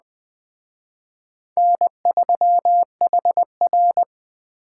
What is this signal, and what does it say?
Morse code, heard as a keyed tone, about 20 wpm, 700 Hz. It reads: N3HR
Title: Speed Racing - USA 1x2 Call Signs 17wpm